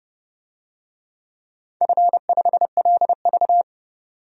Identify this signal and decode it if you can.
Morse code, heard as a keyed tone, about 32 wpm, 700 Hz. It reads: F5LV
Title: DXCC-100 Call Signs - 4 Characters - 30wpm